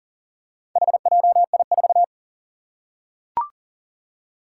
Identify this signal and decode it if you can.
Morse code, heard as a keyed tone, about 40 wpm, 700 Hz. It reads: HJI4
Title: Four Challenging Letter-Number Combinations 40wpm